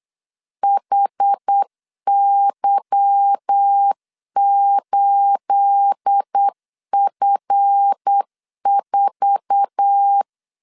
Morse code, heard as a keyed tone, about 8 wpm, 800 Hz. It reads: HY8F4
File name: LEZIONI / 9 28 8wpm 3512